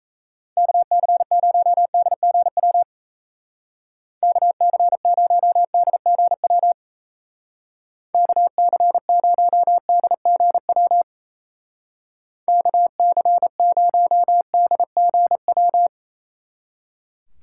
Morse code, heard as a keyed tone, about 37 wpm, 700 Hz. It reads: KC0DGW KC0DGW KC0DGW KC0DGW